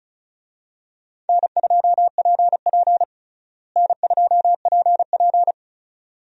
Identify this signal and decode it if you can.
Morse code, heard as a keyed tone, about 36 wpm, 700 Hz. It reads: N2PP N2PP
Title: USA Call Signs - 4 Characters - Repeated 2x - 35wpm